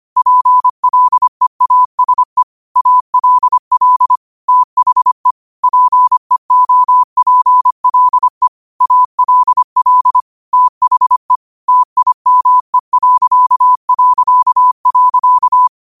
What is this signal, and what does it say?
Morse code, heard as a keyed tone, about 25 wpm, 1000 Hz. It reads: PLEASE ALL THE PEOPLE ALL THE TIME...
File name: qotdfiles / 25wpm